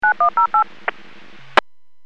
DTMF, heard as telephone touch tones, tones of about 110 ms. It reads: 9108
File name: WPS / 01602 09-04 WPS 451.25 WPS